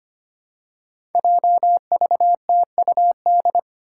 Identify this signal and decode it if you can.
Morse code, heard as a keyed tone, about 25 wpm, 700 Hz. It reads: JVTUD
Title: Five Challenging Letter-Number Combinations 25wpm